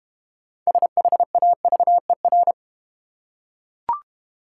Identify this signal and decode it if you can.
Morse code, heard as a keyed tone, about 32 wpm, 690 Hz. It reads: SHAVER